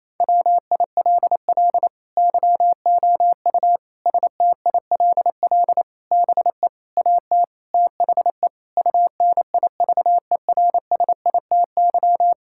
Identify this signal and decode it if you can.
Morse code, heard as a keyed tone, about 28 wpm, 700 Hz. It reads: WILL YOU STILL BE AT THE UNIVERSITY
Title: Sentences from Top 1000 Words - Review Part 5 - 28wpm